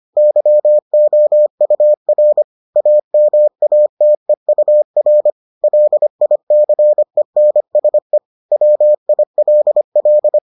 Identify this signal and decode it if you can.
Morse code, heard as a keyed tone, about 25 wpm, 600 Hz. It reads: YOUR AMATEUR LICENSE WILL